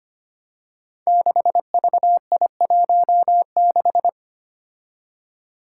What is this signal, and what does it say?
Morse code, heard as a keyed tone, about 25 wpm, 700 Hz. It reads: 6VI16
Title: Five Challenging Letter-Number Combinations 25wpm